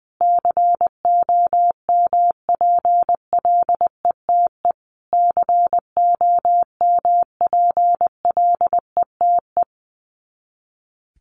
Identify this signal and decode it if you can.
Morse code, heard as a keyed tone, about 19 wpm, 700 Hz. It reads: COMPLETE COMPLETE
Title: Top 500 Words Repeated 20wpm